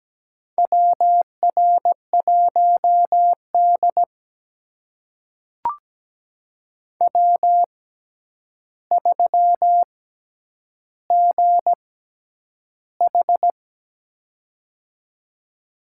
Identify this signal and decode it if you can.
Morse code, heard as a keyed tone, about 17 wpm, spaced as for 9 wpm, 700 Hz. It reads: WR1D W3GH
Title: USA Call Signs - 4 Characters - 6x Character Spacing - 17wpm